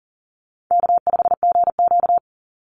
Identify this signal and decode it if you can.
Morse code, heard as a keyed tone, about 40 wpm, 700 Hz. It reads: K5GQ
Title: Easy Call Signs Repeated 2x - 40wpm